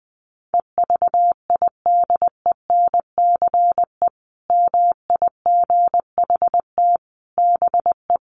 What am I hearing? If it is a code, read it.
Morse code, heard as a keyed tone, about 20 wpm, 700 Hz. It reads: EVIDENCE MIGHT BE
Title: Sentences from Top 1300 Words 20wpm